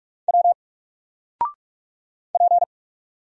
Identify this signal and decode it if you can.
Morse code, heard as a keyed tone, about 44 wpm, 700 Hz. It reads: W P
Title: Single Letter-Number 45wpm